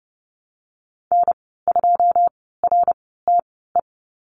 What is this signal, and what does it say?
Morse code, heard as a keyed tone, about 30 wpm, spaced as for 17 wpm, 700 Hz. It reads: N2RTE
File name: Lesson 19g - Letter w - Call Signs 30wpm